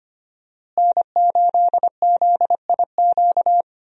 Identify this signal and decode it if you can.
Morse code, heard as a keyed tone, about 25 wpm, 700 Hz. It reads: N8ZIQ
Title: USA Call Signs - 5 Characters - Repeated 2x - 25wpm